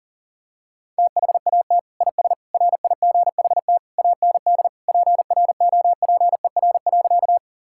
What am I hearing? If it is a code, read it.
Morse code, heard as a keyed tone, about 40 wpm, 700 Hz. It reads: THAT IS RIGHT AND PROPER.
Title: Sentences from Top 1000 Words - Review Part 5 - 40wpm